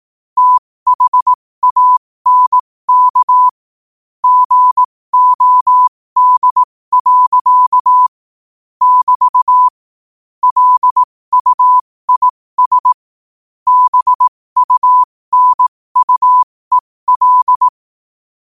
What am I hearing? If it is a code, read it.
Morse code, heard as a keyed tone, about 18 wpm, spaced as for 15 wpm, 1000 Hz. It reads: THANK GOD. <BT> LUIS BUNUEL
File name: qotdfiles / 15wpm